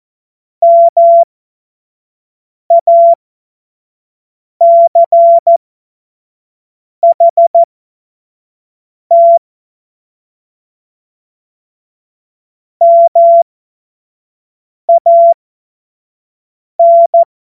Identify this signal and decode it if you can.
Morse code, heard as a keyed tone, about 14 wpm, spaced as for 5 wpm, 680 Hz. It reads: MACHT MAN
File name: Morsekurs / Witze / Witz 24.Oktober 2023 25 F70